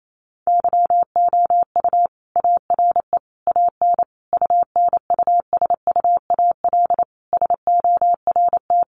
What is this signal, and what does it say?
Morse code, heard as a keyed tone, about 28 wpm, 700 Hz. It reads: YOU ARE AN UNUSUAL SORT
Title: Sentences from Top 1400 Words 28wpm